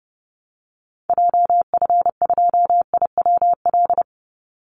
Morse code, heard as a keyed tone, about 30 wpm, 700 Hz. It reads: JF2IWL
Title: Call Signs Repeated 2x - 30wpm